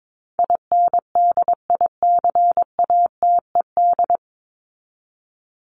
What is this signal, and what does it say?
Morse code, heard as a keyed tone, about 22 wpm, 700 Hz. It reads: INDICATED
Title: ED Suffix - Top 100 Words with Base Words 22wpm